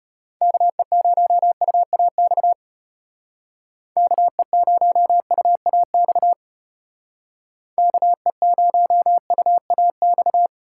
Morse code, heard as a keyed tone, about 33 wpm, 700 Hz. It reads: KE0UAX KE0UAX KE0UAX